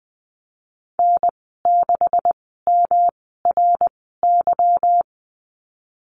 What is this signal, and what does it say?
Morse code, heard as a keyed tone, about 20 wpm, spaced as for 15 wpm, 700 Hz. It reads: N6MRY
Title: 147 Lesson 24g - Letter y - Call Signs 20wpm